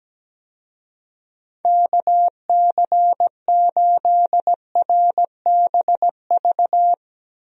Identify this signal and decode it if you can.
Morse code, heard as a keyed tone, about 17 wpm, 700 Hz. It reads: KC8RBV
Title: USA Call Signs - 6 Characters - Repeated 2x - 17wpm